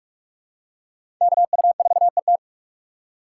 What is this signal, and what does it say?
Morse code, heard as a keyed tone, about 46 wpm, 700 Hz. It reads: KU4ET
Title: USA Call Signs - 5 Characters - 2x Character Spacing - 45wpm